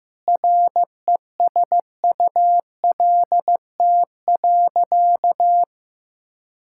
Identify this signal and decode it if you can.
Morse code, heard as a keyed tone, about 15 wpm, 700 Hz. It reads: RESULT.
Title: Sentences from Top 1000 Words - Review Part 5 - 15wpm